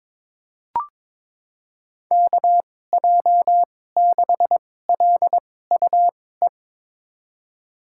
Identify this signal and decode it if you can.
Morse code, heard as a keyed tone, about 22 wpm, spaced as for 16 wpm, 700 Hz. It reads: KJ6LUE
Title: USA Call Signs - 2x Character Spacing - 22wpm